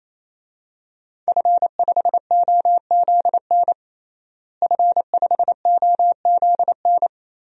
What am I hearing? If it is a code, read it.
Morse code, heard as a keyed tone, about 28 wpm, 700 Hz. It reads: F5OZN F5OZN
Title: DXCC-100 Call Signs - 5 Characters - Repeated 3x - 28wpm